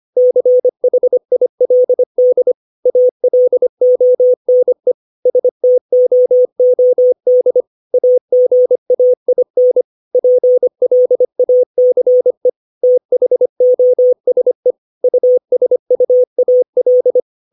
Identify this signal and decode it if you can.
Morse code, heard as a keyed tone, about 25 wpm, 500 Hz. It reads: CHILD ALONE STOOD AGAIN PLACE THOSE USUAL
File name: 632 words / 632 Words 25WPM